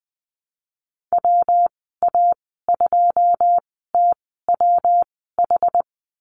Morse code, heard as a keyed tone, about 20 wpm, spaced as for 14 wpm, 700 Hz. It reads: WA2TWH
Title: 154 Lesson 25g - Letter p - Call Signs 20wpm